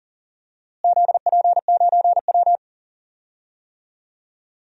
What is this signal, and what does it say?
Morse code, heard as a keyed tone, about 40 wpm, 700 Hz. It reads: ZP9W
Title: DXCC-100 Call Signs - 4 Characters - Repeated 2x - 40wpm